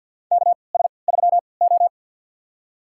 Morse code, heard as a keyed tone, about 50 wpm, spaced as for 28 wpm, 700 Hz. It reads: KS3O
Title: USA Call Signs - 4 Characters - 3x Character Spacing - 50wpm